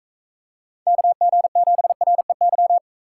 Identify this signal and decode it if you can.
Morse code, heard as a keyed tone, about 36 wpm, 700 Hz. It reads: KG7REY